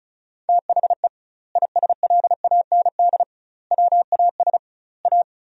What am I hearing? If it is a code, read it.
Morse code, heard as a keyed tone, about 34 wpm, 700 Hz. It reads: THE ISLAND WAS A
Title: 2x Word-Spacing - Sentences from Top 500 Words - 35wpm